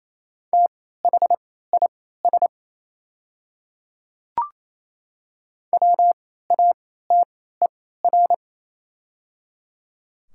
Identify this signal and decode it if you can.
Morse code, heard as a keyed tone, about 27 wpm, spaced as for 16 wpm, 690 Hz. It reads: THIS WATER
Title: 3x Character Spacing - Top 100 Words - 28wpm